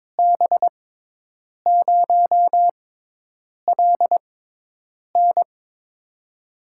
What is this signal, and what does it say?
Morse code, heard as a keyed tone, about 23 wpm, spaced as for 8 wpm, 700 Hz. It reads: B0LN
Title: USA Call Signs - 5 Characters - 6x Character Spacing - 22wpm